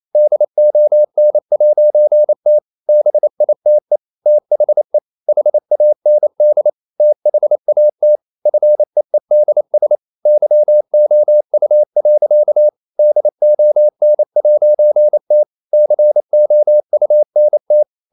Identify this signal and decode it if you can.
Morse code, heard as a keyed tone, about 28 wpm, 600 Hz. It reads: DON'T BITE THE HAND THAT FEEDS YOU. DON'T COUNT
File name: CW / Practice / EnglishSayings28